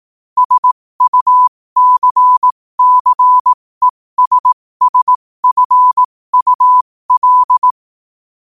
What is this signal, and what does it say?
Morse code, heard as a keyed tone, about 18 wpm, spaced as for 15 wpm, 1000 Hz. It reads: SUCCESSFUL
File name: qotdfiles / 15wpm